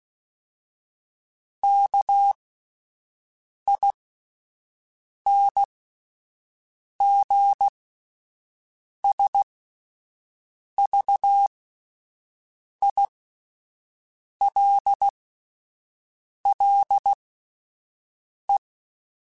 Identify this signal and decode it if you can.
Morse code, heard as a keyed tone, about 16 wpm, spaced as for 5 wpm, 800 Hz. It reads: KINGSVILLE